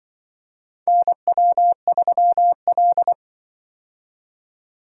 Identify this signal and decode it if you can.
Morse code, heard as a keyed tone, about 24 wpm, 700 Hz. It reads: NW3L
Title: Speed-Racing - Four Letter-Number Combinations - 20wpm